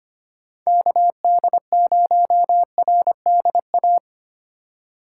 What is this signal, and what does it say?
Morse code, heard as a keyed tone, about 25 wpm, 700 Hz. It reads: KD0RDA
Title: USA Call Signs - 6 Characters - 4x Character Spacing - 25wpm